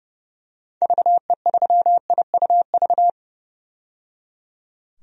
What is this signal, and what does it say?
Morse code, heard as a keyed tone, about 30 wpm, 700 Hz. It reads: VE3IUV
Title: DXCC-100 Call Signs - 6 Characters - 30wpm